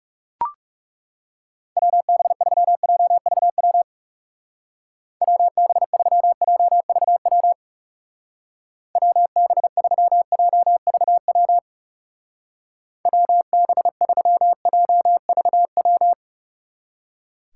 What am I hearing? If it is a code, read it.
Morse code, heard as a keyed tone, about 36 wpm, 700 Hz. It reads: WB3JVW WB3JVW WB3JVW WB3JVW
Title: Speed Racing - USA Call Signs 30wpm